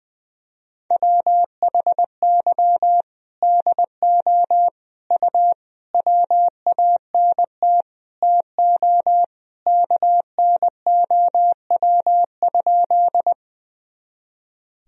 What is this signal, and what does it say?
Morse code, heard as a keyed tone, about 20 wpm, 700 Hz. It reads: WHY DO U WANT TO KNOW?
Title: Sentences from Top 200 Words and Abbreviations 20wpm